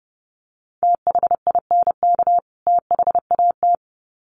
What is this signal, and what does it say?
Morse code, heard as a keyed tone, about 30 wpm, 700 Hz. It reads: THINK THAT
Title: Sets of 2 Words 30wpm